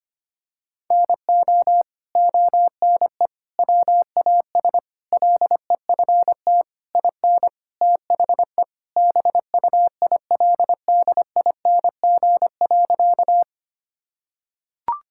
Morse code, heard as a keyed tone, about 25 wpm, 700 Hz. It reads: NO ONE WAS LEFT IN THE BUILDING.
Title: Sentences from Top 1000 Words - Review Part 5 - 25wpm